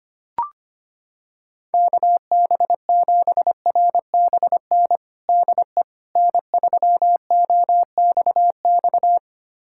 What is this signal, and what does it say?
Morse code, heard as a keyed tone, about 25 wpm, 700 Hz. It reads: KB7RBN DE N3OXX
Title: USA Callsign DE Callsign 25wpm